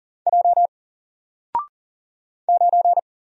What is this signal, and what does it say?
Morse code, heard as a keyed tone, about 40 wpm, 700 Hz. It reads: J 9
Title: Single Letter-Number Alt 40wpm